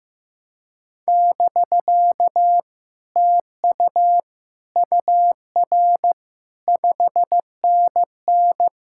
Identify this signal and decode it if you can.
Morse code, heard as a keyed tone, about 15 wpm, 700 Hz. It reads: <BK> TU UR 5NN
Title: Speed-Racing Common QSO Phrases Encore - 15wpm